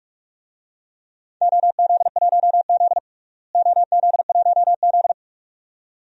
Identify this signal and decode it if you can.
Morse code, heard as a keyed tone, about 42 wpm, 700 Hz. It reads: OZ1Z OZ1Z
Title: DXCC-100 Call Signs - 4 Characters - Repeated 2x - 45wpm